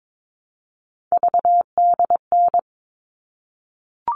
Morse code, heard as a keyed tone, about 22 wpm, 700 Hz. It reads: VDN
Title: All Cut Numbers - 3-Digit 22wpm